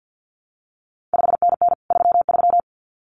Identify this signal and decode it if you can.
Morse code, heard as a keyed tone, about 50 wpm, 700 Hz. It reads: 5NN 33
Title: CQ WW DX Contest Exchange - 50wpm